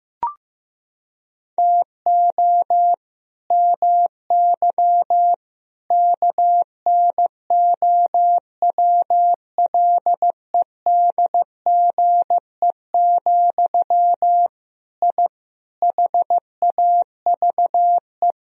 Morse code, heard as a keyed tone, about 15 wpm, 700 Hz. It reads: TO MY KNOWLEDGE, I HAVE
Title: Sentences from Top 1000 Words - Review Part 3 - 15wpm